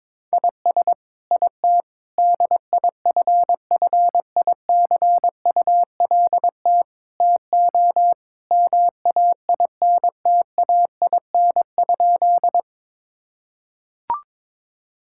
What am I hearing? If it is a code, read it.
Morse code, heard as a keyed tone, about 22 wpm, 700 Hz. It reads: IS IT DIFFICULT TO MAINTAIN?
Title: Sentences from Top 1000 Words - Review Part 5 - 22wpm